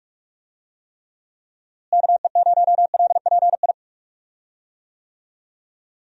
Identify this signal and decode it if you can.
Morse code, heard as a keyed tone, about 42 wpm, 700 Hz. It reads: KE0LPI
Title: USA Call Signs - 6 Characters - Repeated 3x - 45wpm